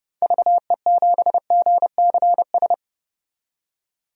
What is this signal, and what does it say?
Morse code, heard as a keyed tone, about 30 wpm, 700 Hz. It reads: VE7GCS
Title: DXCC-100 Call Signs - 6 Characters - 30wpm